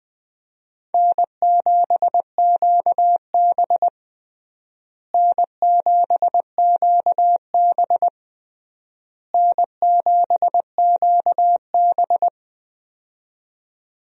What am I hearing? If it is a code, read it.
Morse code, heard as a keyed tone, about 20 wpm, 700 Hz. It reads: N7QB N7QB N7QB
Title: USA Call Signs - 4 Characters - Repeated 3x - 20wpm